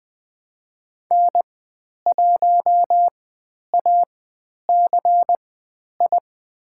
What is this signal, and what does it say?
Morse code, heard as a keyed tone, about 20 wpm, spaced as for 10 wpm, 700 Hz. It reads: N1ACI
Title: 102 Lesson 17g - Letter u - Call Signs 20wpm